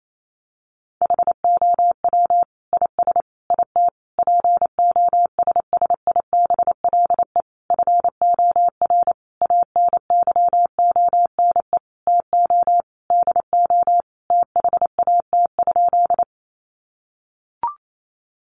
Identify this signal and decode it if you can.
Morse code, heard as a keyed tone, about 28 wpm, 700 Hz. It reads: HOW IS IT POSSIBLE FOR ANYONE TO DO THAT?
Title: Sentences from Top 900 Words 28wpm